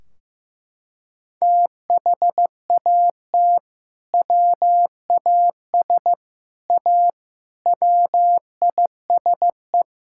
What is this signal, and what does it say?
Morse code, heard as a keyed tone, about 15 wpm, 700 Hz. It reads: THAT WAS A WISE
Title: Sentences from Top 1600 Words 15wpm